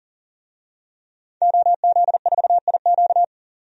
Morse code, heard as a keyed tone, about 40 wpm, 700 Hz. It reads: OZ4IQ
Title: DXCC-100 Call Signs - 5 Characters - 40wpm